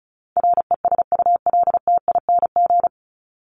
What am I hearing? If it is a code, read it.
Morse code, heard as a keyed tone, about 34 wpm, 700 Hz. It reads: RESULTING
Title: ING Suffix - Top 100 Words with Base Words 35wpm